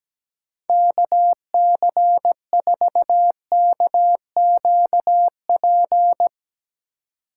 Morse code, heard as a keyed tone, about 17 wpm, 700 Hz. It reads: KC4KQP
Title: Speed Racing - USA Call Signs 17wpm